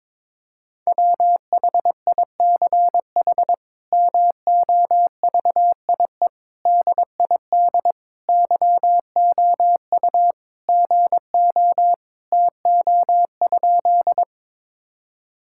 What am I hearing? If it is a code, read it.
Morse code, heard as a keyed tone, about 22 wpm, 700 Hz. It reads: WHICH MOVIE DID YOU GO TO?
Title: Sentences from Top 1000 Words - Review Part 5 - 22wpm